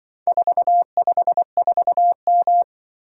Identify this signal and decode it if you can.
Morse code, heard as a keyed tone, about 23 wpm, 700 Hz. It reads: 454M
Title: Speed-Racing - Four Letter-Number Combinations - 20wpm